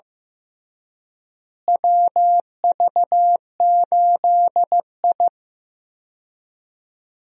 Morse code, heard as a keyed tone, about 15 wpm, 700 Hz. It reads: WV8I
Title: USA Call Signs - 4 Characters - Repeated 3x - 15wpm